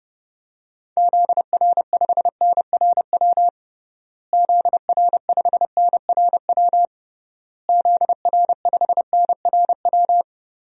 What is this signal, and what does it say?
Morse code, heard as a keyed tone, about 30 wpm, 700 Hz. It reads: ZR5NRW ZR5NRW ZR5NRW
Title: DXCC-100 Call Signs - 6 Characters - Repeated 3x - 30wpm